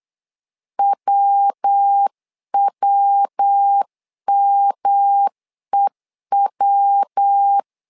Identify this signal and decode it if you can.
Morse code, heard as a keyed tone, about 8 wpm, 800 Hz. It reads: WWMEW
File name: LEZIONI / 3 MWRO / 3 MWR 8wpm 2444